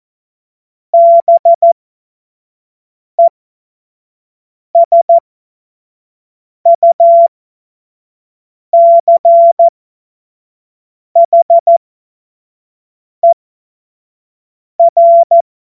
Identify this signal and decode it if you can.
Morse code, heard as a keyed tone, about 14 wpm, spaced as for 5 wpm, 680 Hz. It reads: BESUCHER
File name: Morsekurs / Witze / Witz 04. Juli 2023 25 F70